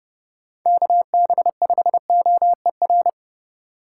Morse code, heard as a keyed tone, about 30 wpm, 700 Hz. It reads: KB5OER